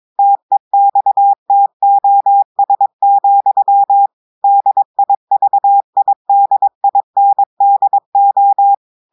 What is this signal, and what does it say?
Morse code, heard as a keyed tone, about 22 wpm, 800 Hz. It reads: TEXTOS, DIVIDINDO
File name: audios / portugais / portugais texte en clair 1 vitesse 22 mots minutes